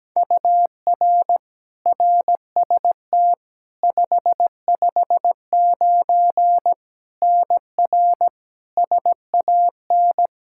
Morse code, heard as a keyed tone, about 17 wpm, 700 Hz. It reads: UR RST 559 NR SAN
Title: Practice QSO - Exchange 1 - Beginning - 17wpm